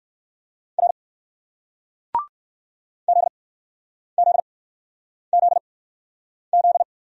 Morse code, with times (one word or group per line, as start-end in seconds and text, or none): 0.8-0.9 R
3.1-6.8 ZZZZ